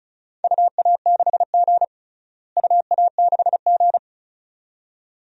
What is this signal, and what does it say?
Morse code, heard as a keyed tone, about 34 wpm, 700 Hz. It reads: UA6G UA6G
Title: DXCC-100 Call Signs - 4 Characters - Repeated 2x - 35wpm